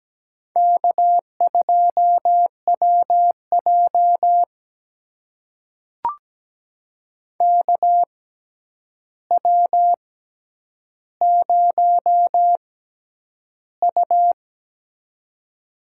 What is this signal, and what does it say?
Morse code, heard as a keyed tone, about 17 wpm, spaced as for 9 wpm, 700 Hz. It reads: K2WJ KW0U
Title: USA Call Signs - 4 Characters - 6x Character Spacing - 17wpm